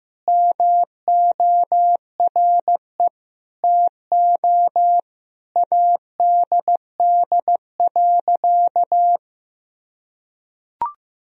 Morse code, heard as a keyed tone, about 15 wpm, 700 Hz. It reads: MORE TO ADD.
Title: Sentences from Top 1000 Words - Review Part 1 - 15wpm